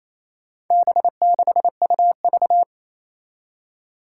Morse code, heard as a keyed tone, about 28 wpm, 700 Hz. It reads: B6UV
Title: Four Challenging Letter-Number Combinations 28wpm